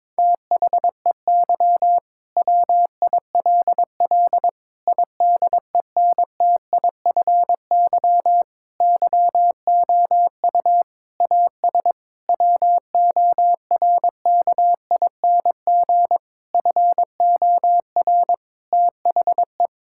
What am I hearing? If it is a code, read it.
Morse code, heard as a keyed tone, about 22 wpm, 700 Hz. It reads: THEY WILL IDENTIFY YOU AS WORKING FOR THE
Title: Sentences from Top 1000 Words - Review Part 6 - 22wpm